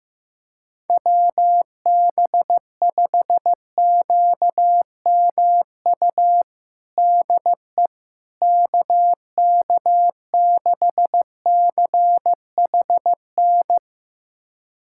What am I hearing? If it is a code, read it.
Morse code, heard as a keyed tone, about 15 wpm, 700 Hz. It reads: WB5QMU DE KK6CHN
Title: USA Callsign DE Callsign 15wpm